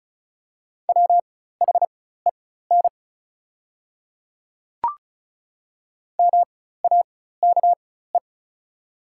Morse code, heard as a keyed tone, about 34 wpm, spaced as for 16 wpm, 690 Hz. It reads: WHEN MAKE
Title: 4x Character Spacing - Top 500 Words - 35wpm